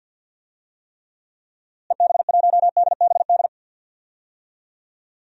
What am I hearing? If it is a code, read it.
Morse code, heard as a keyed tone, about 52 wpm, 700 Hz. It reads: EB1DBD
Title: DXCC-100 Call Signs - 6 Characters - Repeated 3x - 50wpm